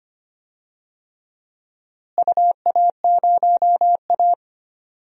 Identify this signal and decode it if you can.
Morse code, heard as a keyed tone, about 25 wpm, 700 Hz. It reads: UA0A
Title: DXCC-100 Call Signs - 4 Characters - Repeated 3x - 25wpm